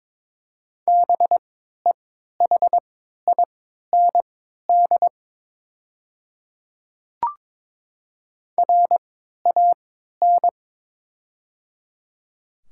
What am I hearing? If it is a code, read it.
Morse code, heard as a keyed tone, about 22 wpm, spaced as for 12 wpm, 700 Hz. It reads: BEHIND RAN
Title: 3x Character Spacing - Top 500 Words - 22wpm